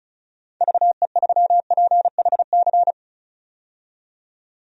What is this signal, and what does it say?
Morse code, heard as a keyed tone, about 36 wpm, 690 Hz. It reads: VE3PHC
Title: DXCC-100 Call Signs - 6 Characters - 35wpm